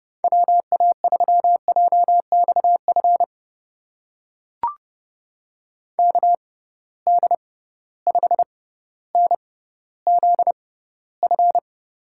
Morse code, heard as a keyed tone, about 30 wpm, spaced as for 15 wpm, 700 Hz. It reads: WA3JXF KD5NZF
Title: USA Call Signs - 6 Characters - 6x Character Spacing - 30wpm